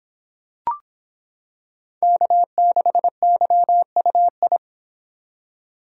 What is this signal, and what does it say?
Morse code, heard as a keyed tone, about 26 wpm, 700 Hz. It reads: K6YUI